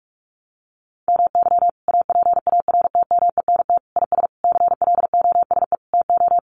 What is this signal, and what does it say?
Morse code, heard as a keyed tone, about 44 wpm, 700 Hz. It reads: MY APARTMENT IS CLOSE TO
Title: Sentences from Top 900 Words 45wpm